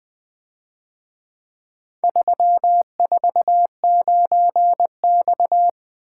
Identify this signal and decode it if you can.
Morse code, heard as a keyed tone, about 24 wpm, 700 Hz. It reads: 349X
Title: Speed-Racing - Four Letter-Number Combinations - 20wpm